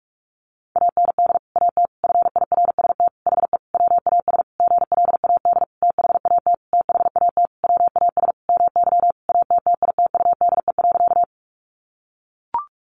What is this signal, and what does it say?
Morse code, heard as a keyed tone, about 44 wpm, 700 Hz. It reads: AND AT FIRST HE WAS GLAD THAT THAT WAS MY ATTITUDE.
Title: Sentences from Top 1300 Words 45wpm